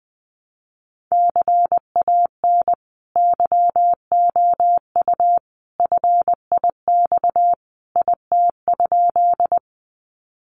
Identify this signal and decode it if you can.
Morse code, heard as a keyed tone, about 19 wpm, 700 Hz. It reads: CAN YOU FIX IT?
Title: Sentences from Top 1400 Words 20wpm